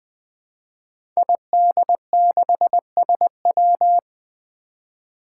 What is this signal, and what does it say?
Morse code, heard as a keyed tone, about 20 wpm, 700 Hz. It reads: ID6SW
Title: Five Challenging Letter-Number Combinations 20wpm